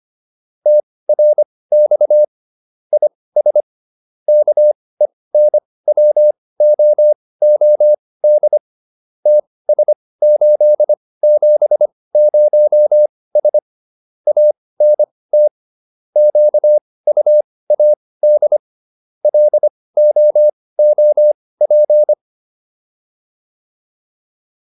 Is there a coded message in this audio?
Yes: Morse code from 0.7 to 22.1 s, heard as a keyed tone, about 25 wpm, spaced as for 18 wpm, 600 Hz. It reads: TRX IS KENWOOD TS870S ANT QUAD LOOP